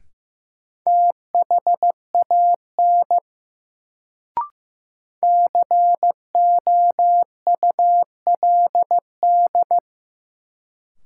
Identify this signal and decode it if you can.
Morse code, heard as a keyed tone, about 15 wpm, 700 Hz. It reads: THAN COULD
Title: Sentences from Top 100 Words - New Word Drill - 15wpm